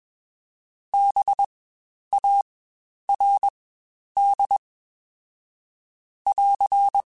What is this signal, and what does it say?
Morse code, heard as a keyed tone, about 21 wpm, spaced as for 10 wpm, 800 Hz. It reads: BARD <AR>
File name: Spaced Alphas / Spaced Alphas mp3s / Bs sp @20